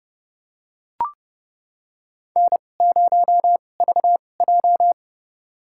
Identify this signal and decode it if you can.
Morse code, heard as a keyed tone, about 30 wpm, spaced as for 22 wpm, 700 Hz. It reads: N0VJ
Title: USA Call Signs - 4 Characters - 2x Character Spacing - 30wpm